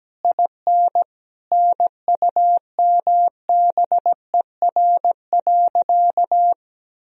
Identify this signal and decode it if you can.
Morse code, heard as a keyed tone, about 17 wpm, 700 Hz. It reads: IN NUMBER.
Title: Sentences from Top 1000 Words - Review Part 4 - 17wpm